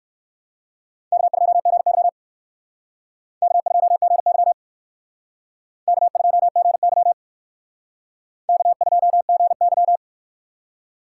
Morse code, heard as a keyed tone, about 59 wpm, 700 Hz. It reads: K2GY K2GY K2GY K2GY